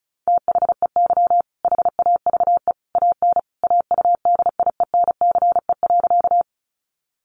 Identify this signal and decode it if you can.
Morse code, heard as a keyed tone, about 34 wpm, 700 Hz. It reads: THEY HAVE AN AUDIENCE.
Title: Sentences from Top 800 Words 35wpm